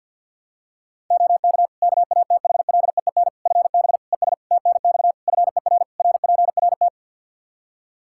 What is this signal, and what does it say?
Morse code, heard as a keyed tone, about 52 wpm, 700 Hz. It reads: OK KATHLEEN FB ES TNX FER RPRT
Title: Practice QSO - Exchange 2 - Info - 50wpm